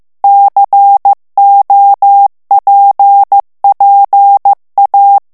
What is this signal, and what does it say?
Morse code, heard as a keyed tone, about 15 wpm, 800 Hz. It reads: COPPA
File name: morse mp3 / ITALIEN texte clair 2 vitesse 14 mots minute